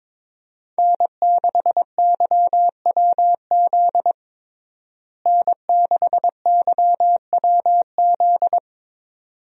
Morse code, heard as a keyed tone, about 22 wpm, 700 Hz. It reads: N6YWZ N6YWZ
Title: USA Call Signs - 5 Characters - Repeated 2x - 22wpm